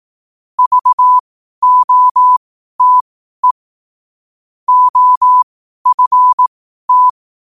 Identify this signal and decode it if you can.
Morse code, heard as a keyed tone, about 18 wpm, spaced as for 12 wpm, 1000 Hz. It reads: VOTE OFT
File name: qotdfiles / 12wpm